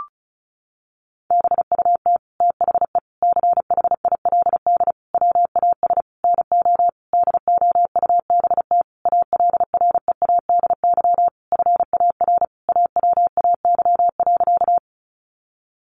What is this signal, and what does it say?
Morse code, heard as a keyed tone, about 34 wpm, 700 Hz. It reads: BUT THE CHILD WAS NO DOUBT ALREADY FAR AWAY.
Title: Sentences from Top 1500 Words 35wpm